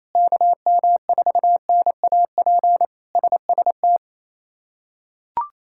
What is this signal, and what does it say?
Morse code, heard as a keyed tone, about 28 wpm, 700 Hz. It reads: KM4NAP SST
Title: K1USN SST QSO Practice - 28wpm